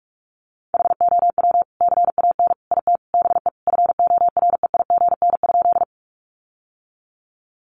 Morse code, heard as a keyed tone, about 44 wpm, 700 Hz. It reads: HOW CAN IT BE FOREIGN?
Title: Sentences from Top 1400 Words 45wpm